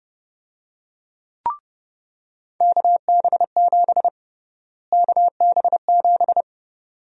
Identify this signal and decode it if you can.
Morse code, heard as a keyed tone, about 32 wpm, 700 Hz. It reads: KB7 KB7
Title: USA Call Sign Prefixes - Repeated 2x - 30wpm